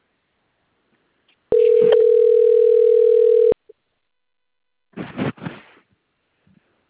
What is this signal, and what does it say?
Ringback tone (North America), 440 + 480 Hz.